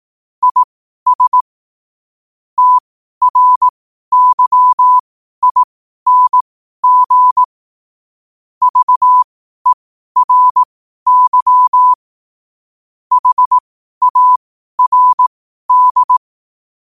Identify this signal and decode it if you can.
Morse code, heard as a keyed tone, about 18 wpm, spaced as for 12 wpm, 1000 Hz. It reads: IS TRYING VERY HARD
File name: qotdfiles / 12wpm